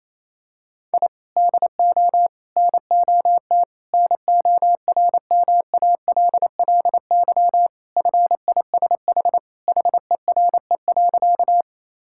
Morse code, heard as a keyed tone, about 28 wpm, 700 Hz. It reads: I DO NOT NORMALLY FISH HERE.
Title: Sentences from Top 1000 Words - Review Part 6 - 28wpm